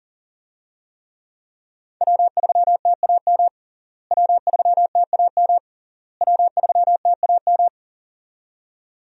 Morse code, heard as a keyed tone, about 40 wpm, 700 Hz. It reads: W3TAM W3TAM W3TAM
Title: USA Call Signs - 5 Characters - Repeated 3x - 40wpm